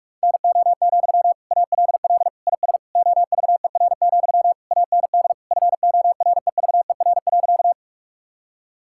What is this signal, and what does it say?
Morse code, heard as a keyed tone, about 46 wpm, 700 Hz. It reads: NO, ALL IS OVER, AND FOREVER.
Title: Sentences from Top 1600 Words 45wpm